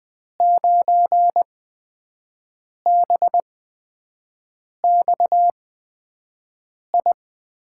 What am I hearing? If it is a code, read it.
Morse code, heard as a keyed tone, about 20 wpm, spaced as for 5 wpm, 700 Hz. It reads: 9BXI
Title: USA Call Signs - 6 Characters - 8x Character Spacing - 20wpm